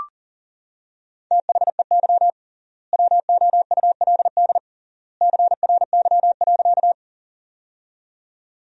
Morse code, heard as a keyed tone, about 40 wpm, 700 Hz. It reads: THEY WOULD CRY.
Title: 3x Word-Spacing - Sentences from Top 500 Words - 40wpm